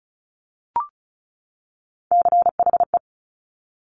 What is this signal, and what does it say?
Morse code, heard as a keyed tone, about 36 wpm, 700 Hz. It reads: CHE
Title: Top Three Letter Combinations in Words 35wpm